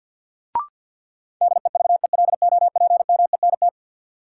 Morse code, heard as a keyed tone, about 48 wpm, 700 Hz. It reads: DEVELOPMENT
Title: Sentences from Top 1000 Words - Review Part 4 - New Word Drill - 50wpm